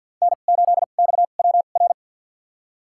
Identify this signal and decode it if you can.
Morse code, heard as a keyed tone, about 52 wpm, spaced as for 37 wpm, 700 Hz. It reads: N8XWR
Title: USA Call Signs - 5 Characters - 2x Character Spacing - 50wpm